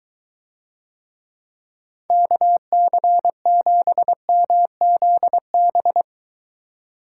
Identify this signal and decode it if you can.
Morse code, heard as a keyed tone, about 23 wpm, 700 Hz. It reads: KC7MZB